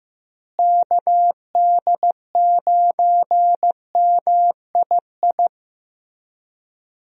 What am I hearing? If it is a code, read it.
Morse code, heard as a keyed tone, about 15 wpm, 700 Hz. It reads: KD9MII
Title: USA Call Signs - 6 Characters - Repeated 2x - 15wpm